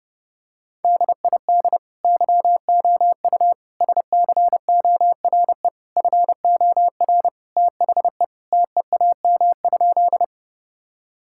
Morse code, heard as a keyed tone, about 32 wpm, 700 Hz. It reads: DID YOU SCORE FOR THE TEAM?
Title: Sentences from Top 1600 Words 30wpm